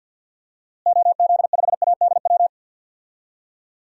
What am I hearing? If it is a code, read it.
Morse code, heard as a keyed tone, about 48 wpm, 700 Hz. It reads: OZ5ADW
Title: DXCC-100 Call Signs - 6 Characters - Repeated 3x - 50wpm